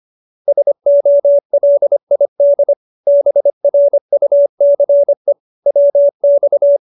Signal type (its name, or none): Morse code